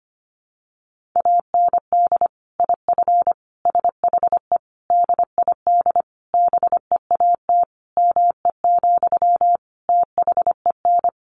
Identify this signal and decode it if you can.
Morse code, heard as a keyed tone, about 25 wpm, 700 Hz. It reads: AND IF SHE DID BEAT ME, THEN
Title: Sentences from Top 1300 Words 25wpm